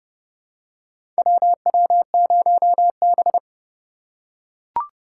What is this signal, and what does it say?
Morse code, heard as a keyed tone, about 32 wpm, 700 Hz. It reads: WW0B
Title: USA Call Signs - 4 Characters - 2x Character Spacing - 30wpm